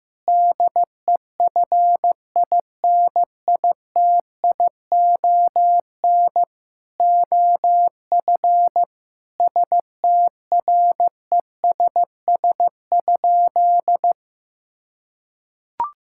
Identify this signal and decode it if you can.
Morse code, heard as a keyed tone, about 15 wpm, 700 Hz. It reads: DEFINITION OF STRESS?
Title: Sentences from Top 1000 Words - Review Part 6 - 15wpm